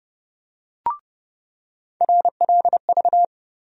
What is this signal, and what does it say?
Morse code, heard as a keyed tone, about 32 wpm, 700 Hz. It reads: RLV
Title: USA Call Sign Postfixes - 3 Letters 30wpm